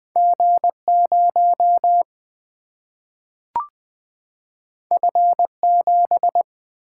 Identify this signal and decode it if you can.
Morse code, heard as a keyed tone, about 20 wpm, 700 Hz. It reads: G0 F7
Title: Single-Letter Single-Number 20wpm